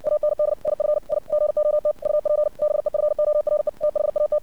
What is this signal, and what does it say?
Morse code, heard as a keyed tone, about 60 wpm, 600 Hz. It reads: AND IF I GOT UP BEFORE THAT
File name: archived / CRSnet